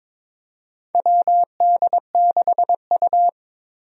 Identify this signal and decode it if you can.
Morse code, heard as a keyed tone, about 22 wpm, 700 Hz. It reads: WD6U
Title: USA Call Signs - 4 Characters - 22wpm